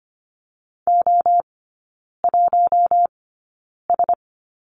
Morse code, heard as a keyed tone, about 25 wpm, spaced as for 9 wpm, 700 Hz. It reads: O1S
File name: Lesson 14e - Number 2 - Three Characters 25wpm